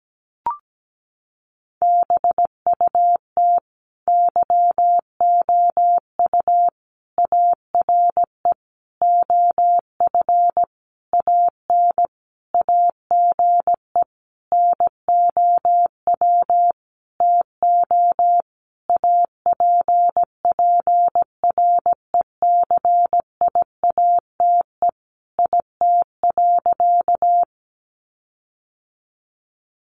Morse code, heard as a keyed tone, about 17 wpm, 700 Hz. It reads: BUT YOU ARE OF AN AGE NOW TO APPRECIATE IT.
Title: Sentences from Top 1300 Words 17wpm